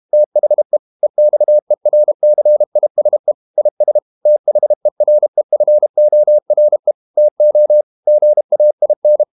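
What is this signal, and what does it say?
Morse code, heard as a keyed tone, about 32 wpm, 600 Hz. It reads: THE EXERCISE IS THEREFORE TO GAIN
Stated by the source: Morse code practice at 32 wpm from CW med Gnister, reading Ken Follett's novel Never in English.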